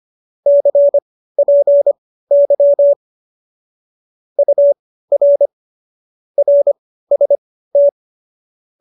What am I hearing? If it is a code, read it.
Morse code, heard as a keyed tone, about 25 wpm, spaced as for 15 wpm, 570 Hz. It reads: CPY UR RST